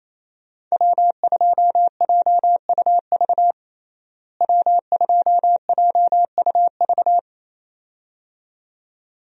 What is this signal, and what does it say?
Morse code, heard as a keyed tone, about 28 wpm, 700 Hz. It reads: W2JUV W2JUV
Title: USA Call Signs - 5 Characters - Repeated 2x - 28wpm